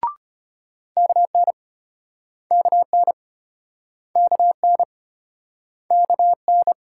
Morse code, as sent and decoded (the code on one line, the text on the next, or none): -.- -. / -.- -. / -.- -. / -.- -.
KN KN KN KN